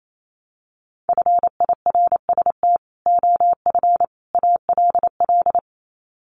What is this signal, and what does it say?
Morse code, heard as a keyed tone, about 28 wpm, 700 Hz. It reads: FIRST OF ALL
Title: Sets of 3 Words 28wpm